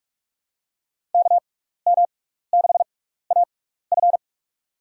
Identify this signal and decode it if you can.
Morse code, heard as a keyed tone, about 44 wpm, spaced as for 15 wpm, 700 Hz. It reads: KM6AF
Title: USA Call Signs - 5 Characters - 6x Character Spacing - 45wpm